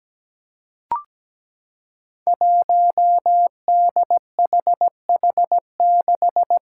Morse code, heard as a keyed tone, about 17 wpm, 700 Hz. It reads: 1DHH6
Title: Five Challenging Letter-Number Combinations 17wpm